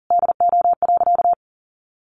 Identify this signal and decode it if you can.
Morse code, heard as a keyed tone, about 40 wpm, 700 Hz. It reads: DO.
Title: Sentences from Top 800 Words 40wpm